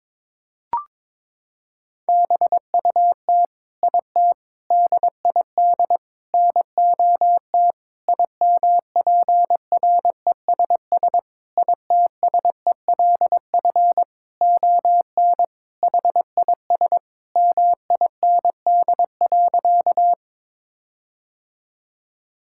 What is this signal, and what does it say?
Morse code, heard as a keyed tone, about 22 wpm, 700 Hz. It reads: BUT IT DID NOT IMPRESS ITSELF ON HIS MIND.
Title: Sentences from Top 1600 Words 22wpm